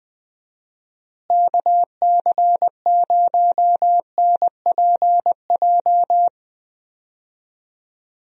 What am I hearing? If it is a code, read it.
Morse code, heard as a keyed tone, about 21 wpm, 700 Hz. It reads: KC0NPJ
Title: USA Call Signs - 6 Characters - Repeated 3x - 20wpm